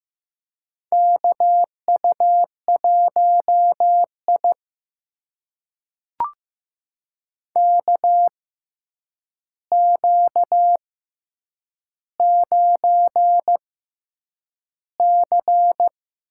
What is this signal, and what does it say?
Morse code, heard as a keyed tone, about 15 wpm, spaced as for 8 wpm, 700 Hz. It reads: KU1I KQ9C
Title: USA Call Signs - 4 Characters - 6x Character Spacing - 15wpm